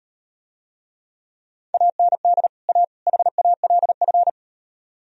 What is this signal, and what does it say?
Morse code, heard as a keyed tone, about 38 wpm, 700 Hz. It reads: AND A HALF